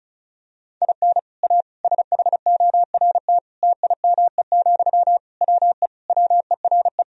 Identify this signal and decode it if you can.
Morse code, heard as a keyed tone, about 36 wpm, 700 Hz. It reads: IN A SHORT TIME, WE WERE
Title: Sentences from Top 1000 Words - Review Part 2 - 35wpm